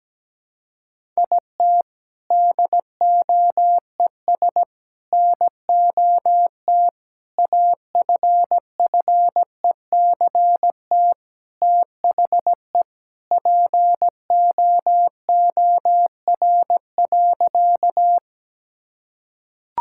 Morse code, heard as a keyed tone, about 17 wpm, 700 Hz. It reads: IT DOES NOT AFFECT THE POOR.
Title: Sentences from Top 1000 Words - Review Part 6 - 17wpm